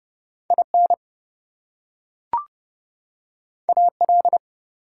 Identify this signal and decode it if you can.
Morse code, heard as a keyed tone, about 30 wpm, 700 Hz. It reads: IN AL
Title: US States and Canadian Provinces 30wpm